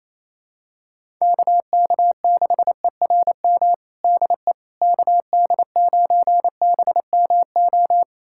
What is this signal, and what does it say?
Morse code, heard as a keyed tone, about 28 wpm, 700 Hz. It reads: KK6ERM DE KD9BMO
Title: USA Callsign DE Callsign - 6 Characters - 28wpm